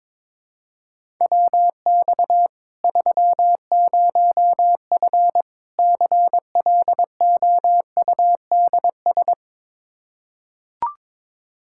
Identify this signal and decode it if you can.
Morse code, heard as a keyed tone, about 22 wpm, 700 Hz. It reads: WX 30F CLOUDS
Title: Practice QSO - Exchange 2 - Info - 22wpm